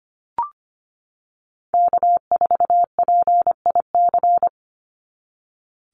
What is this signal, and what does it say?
Morse code, heard as a keyed tone, about 25 wpm, 700 Hz. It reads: K4PIC
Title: NA SOTA Chaser Callsigns 25wpm